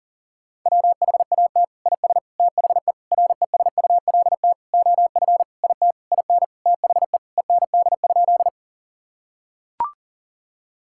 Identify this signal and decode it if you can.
Morse code, heard as a keyed tone, about 40 wpm, 700 Hz. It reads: WHAT IS THE RESULT OF IT IN THE END?
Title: Sentences from Top 1000 Words - Review Part 5 - 40wpm